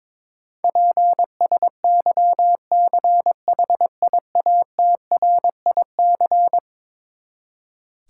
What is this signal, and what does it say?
Morse code, heard as a keyed tone, about 22 wpm, 700 Hz. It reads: PSYCHIATRIC
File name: practice / 22wpm / Eleven Letter Words - 22wpm